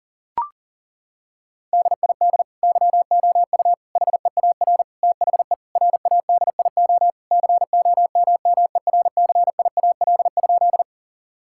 Morse code, heard as a keyed tone, about 40 wpm, 700 Hz. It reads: DID YOU HEAR THE RADIO COMMERCIAL?
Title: Sentences from Top 1000 Words - Review Part 5 - 40wpm